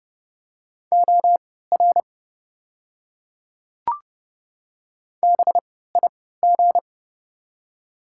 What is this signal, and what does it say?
Morse code, heard as a keyed tone, about 30 wpm, spaced as for 17 wpm, 700 Hz. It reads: OR BIG
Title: 3x Character Spacing - Top 300 Words - 30wpm